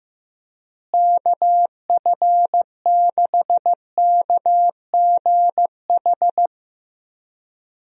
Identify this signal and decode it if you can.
Morse code, heard as a keyed tone, about 15 wpm, 700 Hz. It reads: KF6KGH
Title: USA Call Signs - 6 Characters - Repeated 3x - 15wpm